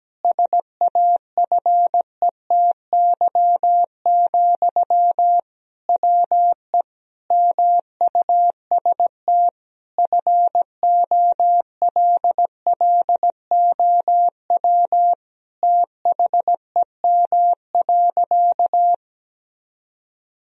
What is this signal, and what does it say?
Morse code, heard as a keyed tone, about 17 wpm, 700 Hz. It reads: SAFETY, WE MUST FOLLOW THEM.
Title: Sentences from Top 1000 Words - Review Part 4 - 17wpm